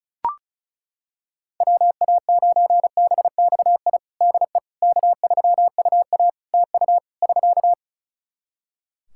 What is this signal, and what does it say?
Morse code, heard as a keyed tone, about 36 wpm, 700 Hz. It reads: WA9BXI DE K3UA TU <SK>
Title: K1USN SST QSO Practice 35wpm